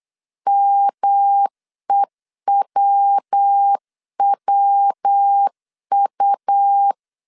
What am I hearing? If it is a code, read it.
Morse code, heard as a keyed tone, about 8 wpm, 800 Hz. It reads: MEWWU